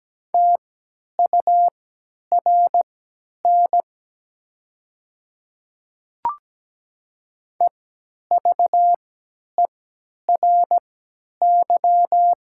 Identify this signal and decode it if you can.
Morse code, heard as a keyed tone, about 17 wpm, spaced as for 10 wpm, 700 Hz. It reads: TURN EVERY
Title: 3x Character Spacing - Top 200 Words - 17wpm